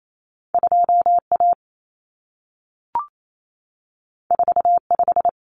Morse code, heard as a keyed tone, about 28 wpm, 700 Hz. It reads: 2A 45
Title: Common Cut Numbers - 2-Digit 28wpm